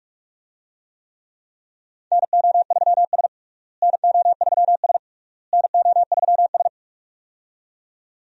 Morse code, heard as a keyed tone, about 42 wpm, 700 Hz. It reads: NO3S NO3S NO3S
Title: USA Call Signs - 4 Characters - Repeated 3x - 45wpm